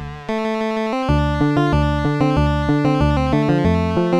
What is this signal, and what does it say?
There is no code.